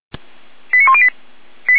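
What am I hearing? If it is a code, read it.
Five-tone selective call (ZVEI), heard as quick runs of five tones, tones of about 70 ms. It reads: 89108
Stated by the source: ZVEI five-tone selective call received by SDR.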